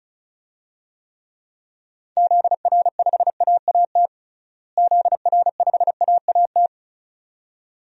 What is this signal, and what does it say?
Morse code, heard as a keyed tone, about 35 wpm, 700 Hz. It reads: ZR5AAT ZR5AAT
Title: DXCC-100 Call Signs - 6 Characters - Repeated 2x - 35wpm